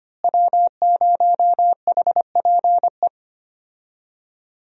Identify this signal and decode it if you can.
Morse code, heard as a keyed tone, about 25 wpm, 700 Hz. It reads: W0HPE
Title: USA Call Signs - 5 Characters - Repeated 3x - 25wpm